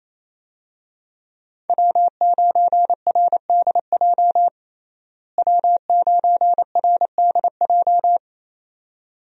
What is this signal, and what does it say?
Morse code, heard as a keyed tone, about 28 wpm, 700 Hz. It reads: W9RDJ W9RDJ
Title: USA Call Signs - 5 Characters - Repeated 2x - 28wpm